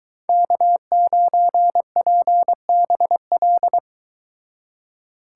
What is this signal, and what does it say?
Morse code, heard as a keyed tone, about 23 wpm, 700 Hz. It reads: K9PBL